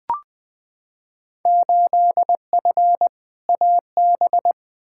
Morse code, heard as a keyed tone, about 20 wpm, 700 Hz. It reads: 8F AB
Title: Field Day Exchanges Encore - 20wpm